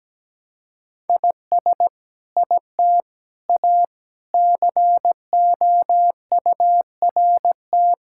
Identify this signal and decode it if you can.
Morse code, heard as a keyed tone, about 17 wpm, 700 Hz. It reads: IS IT A COURT
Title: Sentences from Top 1600 Words 17wpm